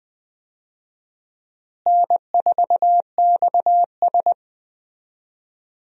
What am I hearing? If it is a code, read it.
Morse code, heard as a keyed tone, about 20 wpm, 700 Hz. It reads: N4XS
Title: USA Call Signs - 4 Characters - Repeated 2x - 20wpm